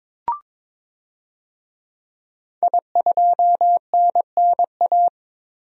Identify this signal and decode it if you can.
Morse code, heard as a keyed tone, about 22 wpm, 700 Hz. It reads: I2NNA
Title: DXCC-100 Call Signs - 5 Characters - 22wpm